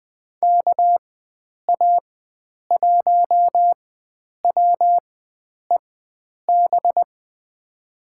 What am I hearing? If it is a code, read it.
Morse code, heard as a keyed tone, about 20 wpm, spaced as for 9 wpm, 700 Hz. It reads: KA1WEB
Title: USA Call Signs - 6 Characters - 4x Character Spacing - 20wpm